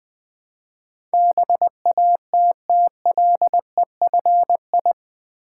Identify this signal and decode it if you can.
Morse code, heard as a keyed tone, about 21 wpm, 700 Hz. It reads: BATTLEFI
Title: Build A Word - 4 Letters At A Time - 11-Letter Words - 20wpm